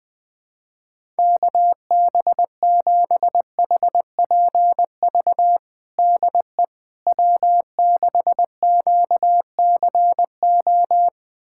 Morse code, heard as a keyed tone, about 20 wpm, 700 Hz. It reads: KB7HPV DE W6QCO
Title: USA Callsign DE Callsign 20wpm